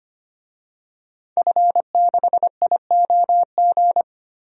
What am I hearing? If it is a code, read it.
Morse code, heard as a keyed tone, about 25 wpm, 700 Hz. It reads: F6IOG
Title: DXCC-100 Call Signs - 5 Characters - Repeated 2x - 25wpm